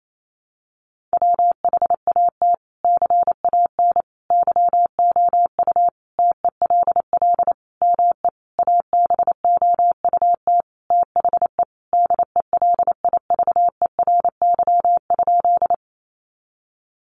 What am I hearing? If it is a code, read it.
Morse code, heard as a keyed tone, about 28 wpm, 700 Hz. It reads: WHAT CAN YOU TELL ME ABOUT THE DELIVERY?
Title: Sentences from Top 1500 Words 28wpm